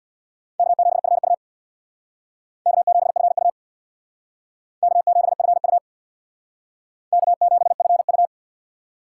Morse code, heard as a keyed tone, about 60 wpm, 700 Hz. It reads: K7FU K7FU K7FU K7FU
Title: Speed Racing - USA 1x2 Call Signs 50wpm